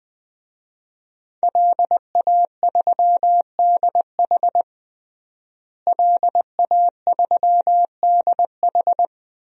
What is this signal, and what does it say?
Morse code, heard as a keyed tone, about 20 wpm, 700 Hz. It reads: LA3DH LA3DH
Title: DXCC-100 Call Signs - 5 Characters - Repeated 2x - 20wpm